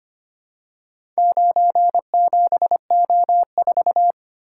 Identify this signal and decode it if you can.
Morse code, heard as a keyed tone, about 25 wpm, 700 Hz. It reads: 97O4
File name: practice / 25wpm / Four Letter-Number Combinations - 25wpm